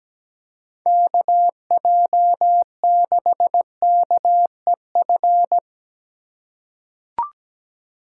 Morse code, heard as a keyed tone, about 17 wpm, 700 Hz. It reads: KJ6KEF
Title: USA Call Signs - 6 Characters - 2x Character Spacing - 17wpm